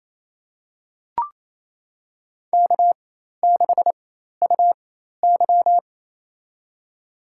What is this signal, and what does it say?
Morse code, heard as a keyed tone, about 28 wpm, spaced as for 13 wpm, 700 Hz. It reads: K6UY
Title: USA Call Signs - 4x Character Spacing - 28wpm